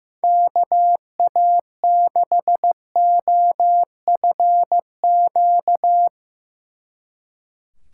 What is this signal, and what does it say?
Morse code, heard as a keyed tone, about 17 wpm, 700 Hz. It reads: KA6OFQ
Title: Speed Racing - USA 2x3 Call Signs 15wpm